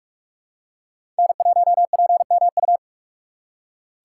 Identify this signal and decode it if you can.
Morse code, heard as a keyed tone, about 46 wpm, 690 Hz. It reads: N1PMU
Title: USA Call Signs - 5 Characters - 6x Character Spacing - 45wpm